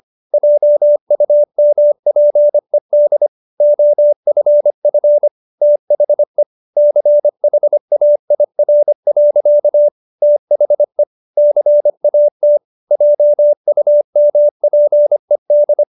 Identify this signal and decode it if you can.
Morse code, heard as a keyed tone, about 25 wpm, 590 Hz. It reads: JUMPED OFF THE CHAIR. THE CAT JUMPED